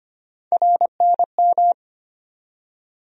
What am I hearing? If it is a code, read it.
Morse code, heard as a keyed tone, about 25 wpm, 700 Hz. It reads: RNM
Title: USA Call Sign Postfixes - 3 Letters 25wpm